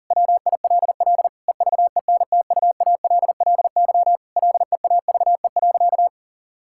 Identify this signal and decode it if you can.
Morse code, heard as a keyed tone, about 40 wpm, 700 Hz. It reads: WILL EVENTUALLY LEAVE.
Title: Sentences from Top 1000 Words - Review Part 6 - 40wpm